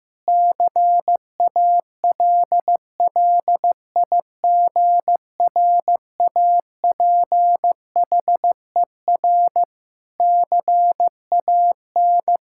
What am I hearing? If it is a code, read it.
Morse code, heard as a keyed tone, about 15 wpm, 700 Hz. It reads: CALLIGRAPHER CAN
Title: Pangram Challenge v2 - 15wpm